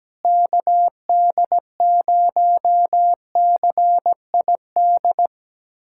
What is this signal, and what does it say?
Morse code, heard as a keyed tone, about 17 wpm, 700 Hz. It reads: KD0CID
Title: USA Callsign DE Callsign 17wpm